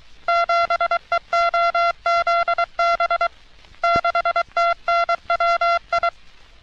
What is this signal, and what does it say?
Morse code, heard as a keyed tone, about 23 wpm, 680 Hz. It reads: 7EOZB 6TNWI
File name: U.S. Navy - RN-2849-124 - Sixteens Morse Code Practice